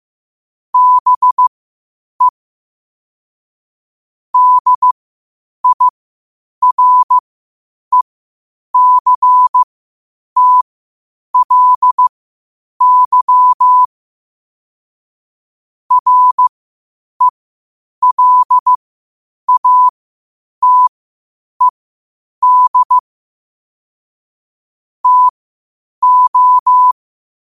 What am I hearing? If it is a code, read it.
Morse code, heard as a keyed tone, about 15 wpm, spaced as for 8 wpm, 1000 Hz. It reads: BE DIRECTLY RELATED TO
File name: qotdfiles / 8wpm